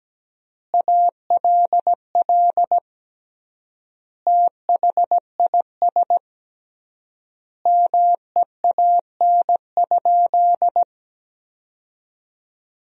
Morse code, heard as a keyed tone, about 17 wpm, 700 Hz. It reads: ALL THIS MEAN?
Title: 3x Word-Spacing - Sentences from Top 200 Words - 17wpm